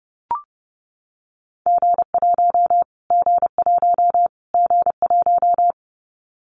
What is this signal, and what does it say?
Morse code, heard as a keyed tone, about 30 wpm, 700 Hz. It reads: G1 G1 G1
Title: Callsign Prefixes for Top 25 DXCC - Repeated 3x - 30wpm